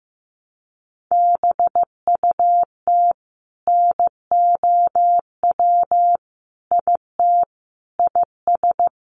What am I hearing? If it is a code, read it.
Morse code, heard as a keyed tone, about 15 wpm, 700 Hz. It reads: BUT NOW IT IS
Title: Sentences from Top 300 Words 15wpm